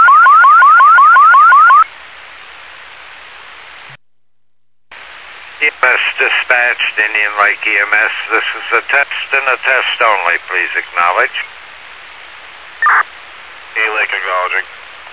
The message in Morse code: none

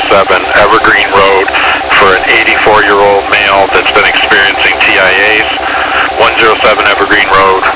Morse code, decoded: none

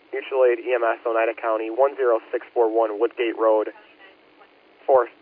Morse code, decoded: none

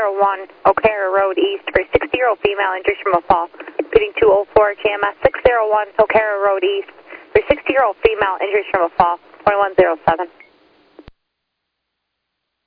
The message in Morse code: none

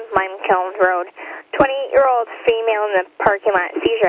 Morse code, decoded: none

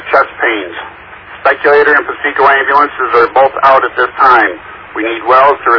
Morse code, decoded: none

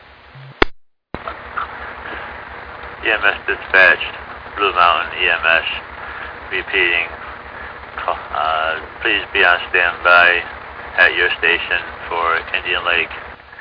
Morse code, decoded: none

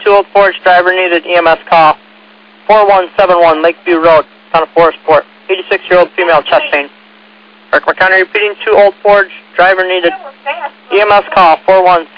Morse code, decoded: none

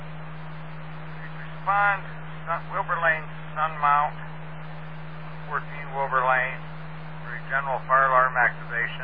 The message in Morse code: none

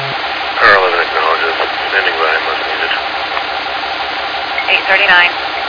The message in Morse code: none

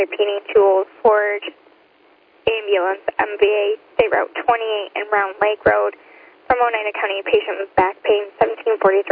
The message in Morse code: none

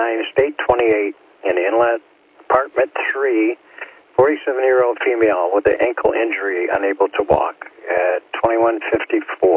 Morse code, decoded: none

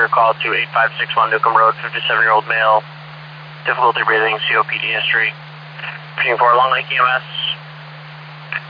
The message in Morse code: none

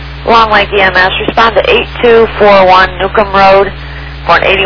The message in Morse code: none

